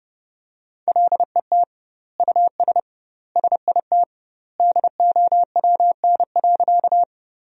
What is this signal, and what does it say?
Morse code, heard as a keyed tone, about 32 wpm, 700 Hz. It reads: LET US SIT DOWN.
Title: 2x Word-Spacing - Sentences from Top 500 Words - 30wpm